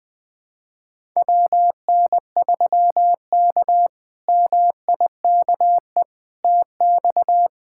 Morse code, heard as a keyed tone, about 20 wpm, 700 Hz. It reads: WN3K MIKE TX
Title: K1USN SST QSO Practice - 20wpm